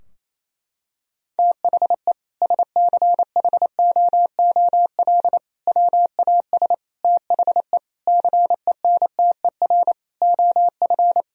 Morse code, heard as a keyed tone, about 28 wpm, 700 Hz. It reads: THE SCHOOL WAS THE CENTER OF
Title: Sentences from Top 1000 Words - Review Part 2 - 28wpm